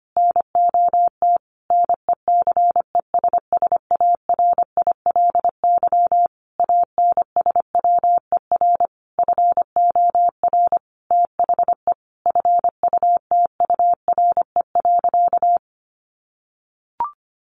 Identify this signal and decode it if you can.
Morse code, heard as a keyed tone, about 25 wpm, 700 Hz. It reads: NOT NECESSARILY ANSWER FOR THE FUTURE.
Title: Sentences from Top 1400 Words 25wpm